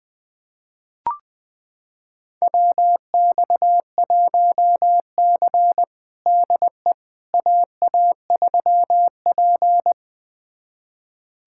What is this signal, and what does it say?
Morse code, heard as a keyed tone, about 21 wpm, 700 Hz. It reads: WX1C DE AA3P
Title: USA Callsign DE Callsign - 4 Characters - 20wpm